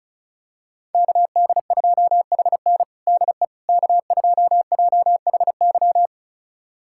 Morse code, heard as a keyed tone, about 36 wpm, 700 Hz. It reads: KD2HN DE K2JHY
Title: USA Callsign DE Callsign - 5 Characters - 35wpm